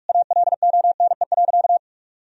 Morse code, heard as a keyed tone, about 46 wpm, 700 Hz. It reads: ALONE.
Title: Sentences from Top 1000 Words - Review Part 2 - 45wpm